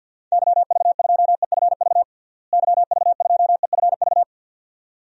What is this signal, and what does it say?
Morse code, heard as a keyed tone, about 47 wpm, 700 Hz. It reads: YV2EFV YV2EFV
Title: DXCC-100 Call Signs - 6 Characters - Repeated 2x - 50wpm